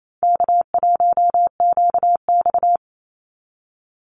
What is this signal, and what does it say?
Morse code, heard as a keyed tone, about 28 wpm, 700 Hz. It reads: K1QX
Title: Easy Call Signs - 28wpm